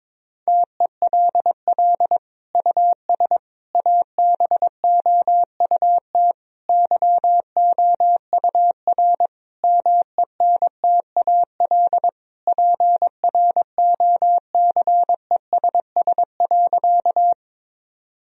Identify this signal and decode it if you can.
Morse code, heard as a keyed tone, about 22 wpm, 700 Hz. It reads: TELL US ABOUT YOUR MENTAL PROCESS.
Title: Sentences from Top 1000 Words - Review Part 4 - 22wpm